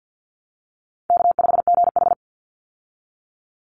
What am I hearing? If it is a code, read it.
Morse code, heard as a keyed tone, about 50 wpm, 700 Hz. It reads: K5GH
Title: Call Signs Repeated 2x - 50wpm